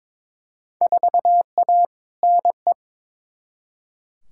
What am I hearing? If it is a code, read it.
Morse code, heard as a keyed tone, about 22 wpm, 700 Hz. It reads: 4A NE
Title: Field Day Exchanges - 22wpm